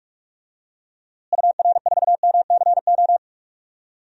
Morse code, heard as a keyed tone, about 43 wpm, 700 Hz. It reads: UR4MCO
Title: DXCC-100 Call Signs - 6 Characters - Repeated 2x - 45wpm